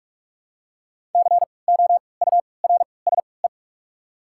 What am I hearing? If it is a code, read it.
Morse code, heard as a keyed tone, about 43 wpm, spaced as for 25 wpm, 690 Hz. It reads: COURSE
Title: 3x Character Spacing - Top 500 Words - 45wpm